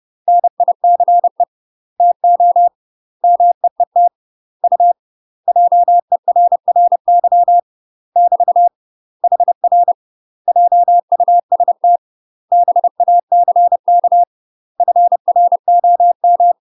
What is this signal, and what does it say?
Morse code, heard as a keyed tone, about 30 wpm, 700 Hz. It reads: NICE TO MEET U JERRY <BT> HR JUST BACK FROM
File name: UK MASTER QSOs TRAINING / 30WPM / UK MASTER QSOs 0026